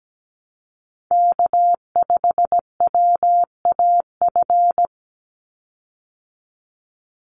Morse code, heard as a keyed tone, about 17 wpm, 700 Hz. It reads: K5WAF
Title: Call Signs Repeated 2x - 17wpm